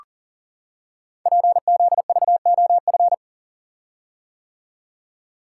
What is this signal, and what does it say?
Morse code, heard as a keyed tone, about 39 wpm, 700 Hz. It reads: PZVOF
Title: Code Groups - 5 Letters - 40wpm